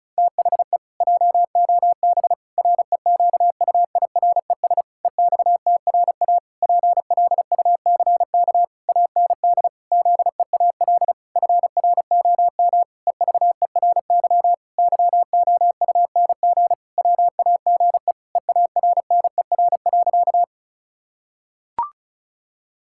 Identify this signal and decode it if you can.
Morse code, heard as a keyed tone, about 34 wpm, 700 Hz. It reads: THE JOB REQUIRES EXTRA PLUCK AND ZEAL FROM EVERY YOUNG WAGE EARNER.
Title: Pangram Challenge v2 - 35wpm